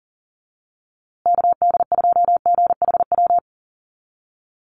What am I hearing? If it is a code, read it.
Morse code, heard as a keyed tone, about 37 wpm, 700 Hz. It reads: KD2GHW
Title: USA Call Signs Repeated 3x 40wpm